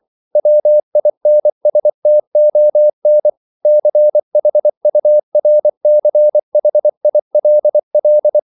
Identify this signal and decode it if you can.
Morse code, heard as a keyed tone, about 24 wpm, 600 Hz. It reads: WINSTON CHURCHILL
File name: CW / Practice / EnglishQuotations0424